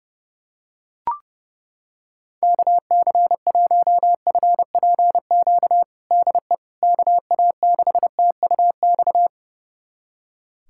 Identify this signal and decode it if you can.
Morse code, heard as a keyed tone, about 28 wpm, 700 Hz. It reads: KC1FPQ DE KA6TUX
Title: USA Callsign DE Callsign 30wpm